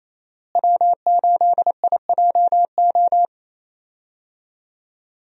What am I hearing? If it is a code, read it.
Morse code, heard as a keyed tone, about 28 wpm, 700 Hz. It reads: W8IJO
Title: USA Call Signs - 5 Characters - Repeated 3x - 28wpm